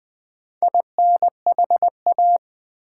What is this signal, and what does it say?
Morse code, heard as a keyed tone, about 20 wpm, 700 Hz. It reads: INHA
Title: Build A Word - 4 Letters At A Time - 10-Letter Words - 20wpm